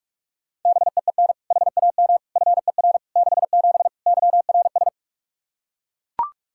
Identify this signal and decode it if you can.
Morse code, heard as a keyed tone, about 41 wpm, 700 Hz. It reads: BEEN HAM FER 67 YRS
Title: Practice QSO - Exchange 3 - More Info - 45wpm